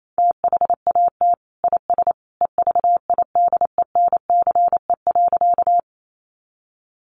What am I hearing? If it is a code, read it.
Morse code, heard as a keyed tone, about 28 wpm, 700 Hz. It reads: THAT IS EVIDENCE.
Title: Sentences from Top 1300 Words 28wpm